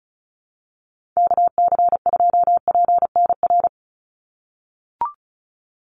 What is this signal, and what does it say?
Morse code, heard as a keyed tone, about 34 wpm, 700 Hz. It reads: KC2PNR
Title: USA Call Signs 35wpm